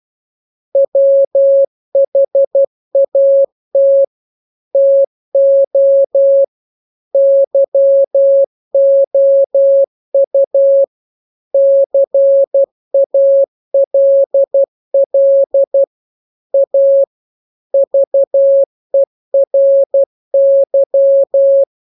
Morse code, heard as a keyed tone, about 12 wpm, 550 Hz. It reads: WHAT TO YOU CALL A VERY
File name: jokes / joke 8 12 0000